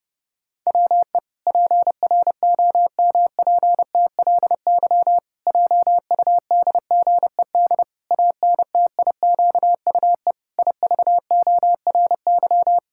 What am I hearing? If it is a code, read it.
Morse code, heard as a keyed tone, about 32 wpm, 700 Hz. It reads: WE PROMPTLY JUDGED ANTIQUE IVORY
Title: Pangram Challenge v2 - 30wpm